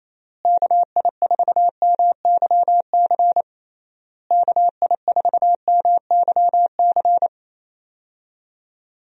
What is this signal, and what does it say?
Morse code, heard as a keyed tone, about 28 wpm, 700 Hz. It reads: KI4MYC KI4MYC
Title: USA Call Signs - 6 Characters - Repeated 3x - 28wpm